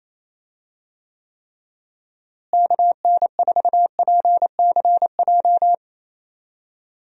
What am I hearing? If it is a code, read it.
Morse code, heard as a keyed tone, about 28 wpm, 700 Hz. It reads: KN4PCJ
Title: USA Call Signs - 6 Characters - Repeated 3x - 28wpm